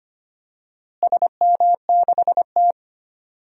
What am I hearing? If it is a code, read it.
Morse code, heard as a keyed tone, about 25 wpm, 700 Hz. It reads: SM6T
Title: DXCC-100 Call Signs - 4 Characters - Repeated 3x - 25wpm